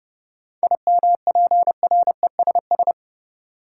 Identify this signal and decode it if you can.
Morse code, heard as a keyed tone, about 30 wpm, 700 Hz. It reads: IMPRESS
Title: Sentences from Top 1600 Words - New Word Drill - 30wpm